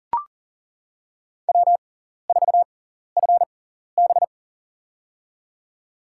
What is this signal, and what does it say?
Morse code, heard as a keyed tone, about 39 wpm, spaced as for 14 wpm, 700 Hz. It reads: W4FB
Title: USA Call Signs - 4 Characters - 6x Character Spacing - 40wpm